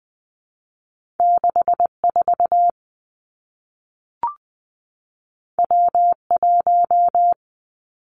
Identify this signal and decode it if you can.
Morse code, heard as a keyed tone, about 20 wpm, 700 Hz. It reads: 6 4 W 1
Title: Two Challenging Letter-Number Combinations 20wpm